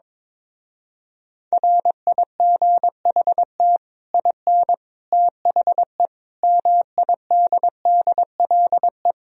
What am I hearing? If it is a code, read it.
Morse code, heard as a keyed tone, about 22 wpm, 700 Hz. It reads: RIGHT IN THE MIDDLE
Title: Speed-Racing - Sets of 5 Words 22wpm